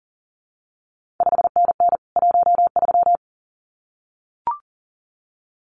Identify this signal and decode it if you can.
Morse code, heard as a keyed tone, about 39 wpm, 700 Hz. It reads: 5NN 13
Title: CQ WW DX Contest Exchange Encore - 40wpm